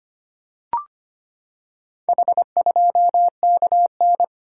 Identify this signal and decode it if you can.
Morse code, heard as a keyed tone, about 25 wpm, 700 Hz. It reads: H2KN
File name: practice / 25wpm / Four Letter-Number Combinations - 25wpm